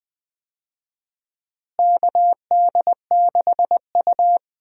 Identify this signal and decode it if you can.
Morse code, heard as a keyed tone, about 20 wpm, 700 Hz. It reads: KD6U
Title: USA Call Signs - 4 Characters - Repeated 3x - 20wpm